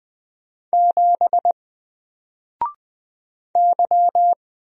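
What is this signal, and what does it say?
Morse code, heard as a keyed tone, about 20 wpm, 700 Hz. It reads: 7 Y
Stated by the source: Single letter and number Morse drill at 20 wpm, from N4LCD.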